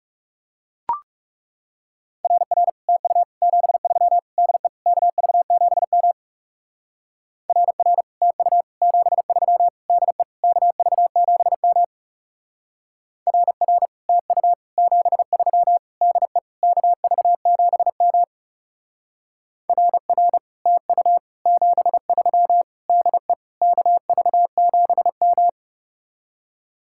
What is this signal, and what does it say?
Morse code, heard as a keyed tone, about 36 wpm, 700 Hz. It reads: RR TU 73 DE KV7M RR TU 73 DE KV7M RR TU 73 DE KV7M RR TU 73 DE KV7M
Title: Speed-Racing Common QSO Phrases Encore - 30wpm